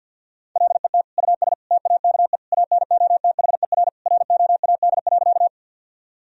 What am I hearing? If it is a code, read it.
Morse code, heard as a keyed tone, about 50 wpm, 700 Hz. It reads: LET US TAKE ANOTHER ROAD.
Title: Sentences from Top 1000 Words - Review Part 2 - 50wpm